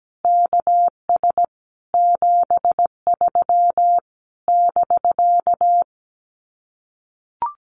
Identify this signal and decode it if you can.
Morse code, heard as a keyed tone, about 17 wpm, 700 Hz. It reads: KS 73 <BK>
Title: Common QSO Phrases Encore 17wpm